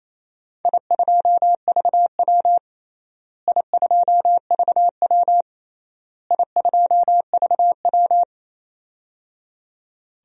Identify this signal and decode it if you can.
Morse code, heard as a keyed tone, about 28 wpm, 700 Hz. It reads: I2VW I2VW I2VW
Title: DXCC-100 Call Signs - 4 Characters - Repeated 3x - 28wpm